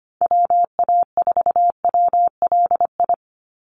Morse code, heard as a keyed tone, about 25 wpm, 700 Hz. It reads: WA4WLI
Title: Call Signs - 25wpm